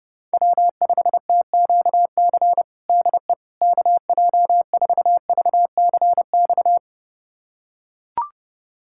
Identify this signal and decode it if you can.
Morse code, heard as a keyed tone, about 30 wpm, 700 Hz. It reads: W5TQC DE KJ4VCX
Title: USA Callsign DE Callsign 30wpm